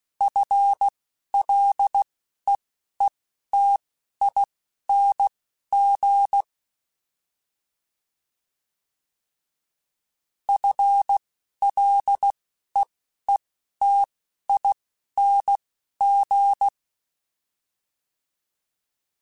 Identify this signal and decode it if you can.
Morse code, heard as a keyed tone, about 16 wpm, spaced as for 12 wpm, 800 Hz. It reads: FLEETING FLEETING
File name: Combinations / Combinations-mp3s / Combinations@10